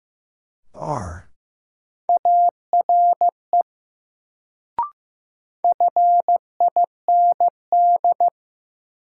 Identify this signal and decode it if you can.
Morse code, heard as a keyed tone, about 15 wpm, 700 Hz. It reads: ARE FIND
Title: Sentences from Top 100 Words - New Word Drill - 15wpm